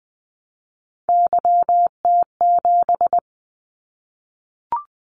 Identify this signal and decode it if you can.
Morse code, heard as a keyed tone, about 19 wpm, 700 Hz. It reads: YT7
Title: Callsign Prefixes for Top 25 DXCC - 20wpm